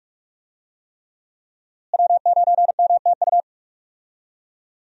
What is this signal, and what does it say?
Morse code, heard as a keyed tone, about 43 wpm, 700 Hz. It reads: W9MTU
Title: USA Call Signs - 5 Characters - Repeated 3x - 45wpm